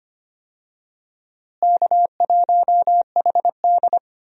Morse code, heard as a keyed tone, about 25 wpm, 700 Hz. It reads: K1HD
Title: USA Call Signs - 4 Characters - Repeated 3x - 25wpm